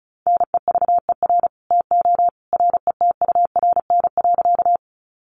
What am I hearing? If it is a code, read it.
Morse code, heard as a keyed tone, about 34 wpm, 700 Hz. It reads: NEVER TO RETURN.
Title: Sentences from Top 800 Words 35wpm